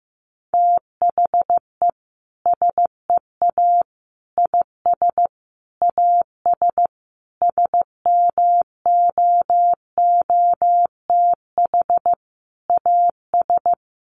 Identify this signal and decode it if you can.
Morse code, heard as a keyed tone, about 15 wpm, 700 Hz. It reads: THE SEA IS AS SMOOTH AS
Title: Sentences from Top 1400 Words 15wpm